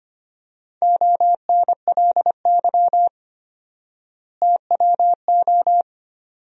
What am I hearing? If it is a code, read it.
Morse code, heard as a keyed tone, about 25 wpm, 700 Hz. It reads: ONLY TWO
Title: 4x Word-Spacing - Sentences from Top 200 Words - 25wpm